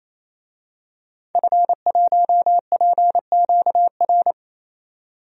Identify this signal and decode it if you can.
Morse code, heard as a keyed tone, about 28 wpm, 700 Hz. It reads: F1PQR
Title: DXCC-100 Call Signs - 5 Characters - 28wpm